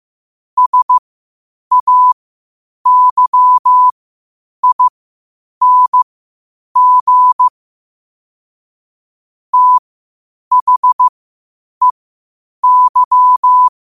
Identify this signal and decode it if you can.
Morse code, heard as a keyed tone, about 15 wpm, spaced as for 8 wpm, 1000 Hz. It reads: SAYING THEY
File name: qotdfiles / 8wpm